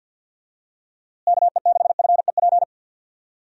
Morse code, heard as a keyed tone, about 53 wpm, 700 Hz. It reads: KE6FEP
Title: USA Call Signs - 6 Characters - Repeated 2x - 50wpm